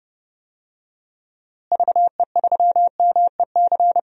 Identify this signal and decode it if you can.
Morse code, heard as a keyed tone, about 30 wpm, 700 Hz. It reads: VE3MEC
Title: DXCC-100 Call Signs - 6 Characters - Repeated 3x - 30wpm